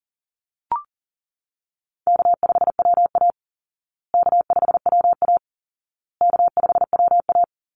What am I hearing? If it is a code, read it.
Morse code, heard as a keyed tone, about 40 wpm, 700 Hz. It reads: K5WA K5WA K5WA
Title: Easy Call Signs Repeated 3x - 40wpm